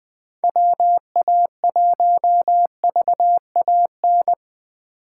Morse code, heard as a keyed tone, about 21 wpm, 700 Hz. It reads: WA1VAN
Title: Speed Racing - USA 2x3 Call Signs 17wpm